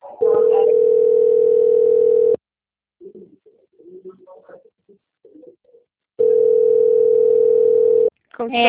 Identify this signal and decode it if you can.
Ringback tone (North America), 440 + 480 Hz, with speech.
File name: out- - - - - .57978